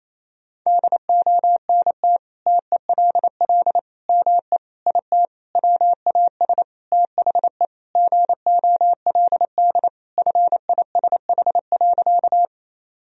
Morse code, heard as a keyed tone, about 28 wpm, 700 Hz. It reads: DONT TELL ME IT WAS THE GOLD FISH.
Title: Sentences from Top 1000 Words - Review Part 3 - 28wpm